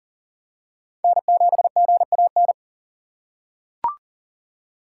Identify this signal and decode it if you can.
Morse code, heard as a keyed tone, about 40 wpm, 700 Hz. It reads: N7GAN
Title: USA Call Signs - 5 Characters - 3x Character Spacing - 40wpm